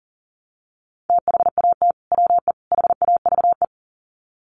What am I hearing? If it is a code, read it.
Morse code, heard as a keyed tone, about 40 wpm, 700 Hz. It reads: THAT WE HAVE
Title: Sets of 3 Words 40wpm